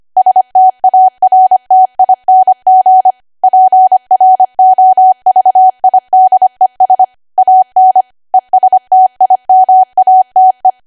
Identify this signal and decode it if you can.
Morse code, heard as a keyed tone, about 25 wpm, 750 Hz. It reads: STARTING PROVIDES AN ESTIMATE
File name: Archive / 25 WPM / 25WPM